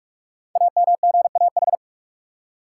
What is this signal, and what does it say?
Morse code, heard as a keyed tone, about 44 wpm, 700 Hz. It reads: AMGAH
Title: Code Groups - 5 Letters - 45wpm